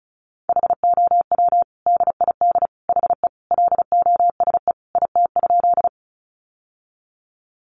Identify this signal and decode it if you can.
Morse code, heard as a keyed tone, about 34 wpm, 700 Hz. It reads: HOW DID HE LOSE IT?
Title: Sentences from Top 1100 Words 35wpm